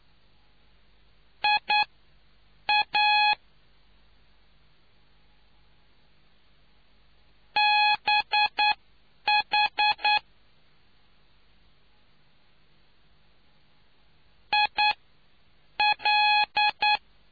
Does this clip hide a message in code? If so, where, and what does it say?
Yes: Morse code from 1.4 to 17.0 s, heard as a keyed tone, about 9 wpm, spaced as for 7 wpm, 840 Hz. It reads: IA BH IL